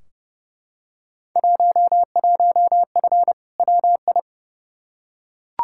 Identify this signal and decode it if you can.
Morse code, heard as a keyed tone, about 30 wpm, 700 Hz. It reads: 11F WI
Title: Field Day Exchanges Encore - 30wpm